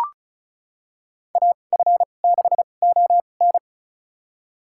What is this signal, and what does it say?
Morse code, heard as a keyed tone, about 34 wpm, spaced as for 25 wpm, 700 Hz. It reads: AF6ON
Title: USA Call Signs - 5 Characters - 2x Character Spacing - 35wpm